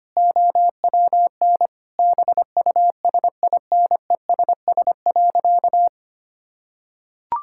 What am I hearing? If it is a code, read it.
Morse code, heard as a keyed tone, about 25 wpm, 700 Hz. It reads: OWN BUSINESS.
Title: Sentences from Top 1000 Words - Review Part 6 - 25wpm